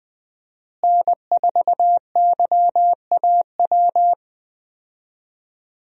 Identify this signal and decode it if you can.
Morse code, heard as a keyed tone, about 20 wpm, 700 Hz. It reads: N4YAW
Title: USA Call Signs - 5 Characters - Repeated 3x - 20wpm